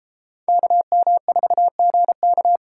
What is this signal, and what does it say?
Morse code, heard as a keyed tone, about 32 wpm, 700 Hz. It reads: KM4GK